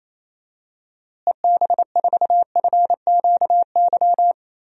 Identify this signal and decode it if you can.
Morse code, heard as a keyed tone, about 27 wpm, 700 Hz. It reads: EB4FQY
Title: DXCC-100 Call Signs - 6 Characters - Repeated 3x - 28wpm